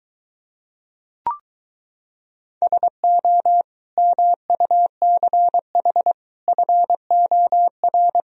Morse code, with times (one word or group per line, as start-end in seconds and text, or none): 2.6-3.6 SO
4.0-6.1 MUCH
6.5-8.2 FOR